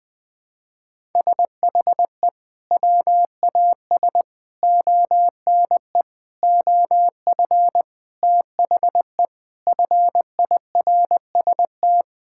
Morse code, heard as a keyed tone, about 20 wpm, 700 Hz. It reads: SHE WAS ONE OF THE FIRST
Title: Sentences with 6-Word Phrases - Spoken Beginning - 20wpm